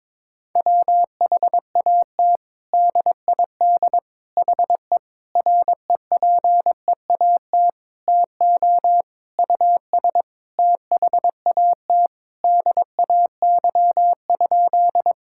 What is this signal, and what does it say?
Morse code, heard as a keyed tone, about 21 wpm, 700 Hz. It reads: WHAT DID HE REPEAT TO US THAT DAY?
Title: Sentences from Top 1600 Words 22wpm